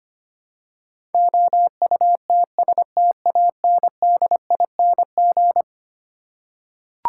Morse code, heard as a keyed tone, about 25 wpm, 700 Hz. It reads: OUTSTANDING
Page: morse-code-ninja-practice-25wpm